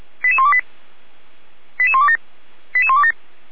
Five-tone selective call (ZVEI), heard as quick runs of five tones, tones of about 70 ms. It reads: 88128 88127 88127